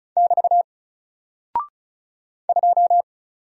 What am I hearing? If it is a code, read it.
Morse code, heard as a keyed tone, about 36 wpm, 700 Hz. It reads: <BT> 2
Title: Single Character (Letter:Number:Punctuation:Prosign) 35wpm